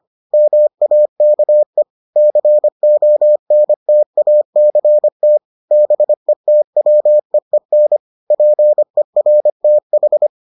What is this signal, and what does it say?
Morse code, heard as a keyed tone, about 25 wpm, 600 Hz. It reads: MAKE CONTACT BETWEEN PERTH